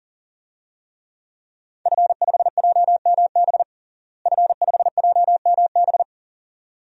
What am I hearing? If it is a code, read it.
Morse code, heard as a keyed tone, about 37 wpm, 700 Hz. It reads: F5JMB F5JMB
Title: DXCC-100 Call Signs - 5 Characters - Repeated 2x - 40wpm